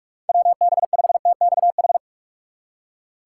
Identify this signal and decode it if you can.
Morse code, heard as a keyed tone, about 46 wpm, 700 Hz. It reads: WB5TXH
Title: USA Call Signs - 6 Characters - 4x Character Spacing - 45wpm